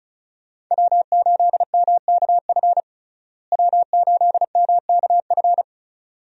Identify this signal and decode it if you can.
Morse code, heard as a keyed tone, about 36 wpm, 700 Hz. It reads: W8MKF W8MKF
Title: USA Call Signs - 5 Characters - Repeated 2x - 35wpm